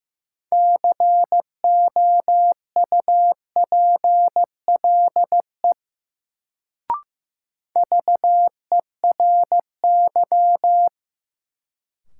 Morse code, heard as a keyed tone, about 15 wpm, 700 Hz. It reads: COUPLE VERY
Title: Sets of 2 Words - New Word Drill - 15wpm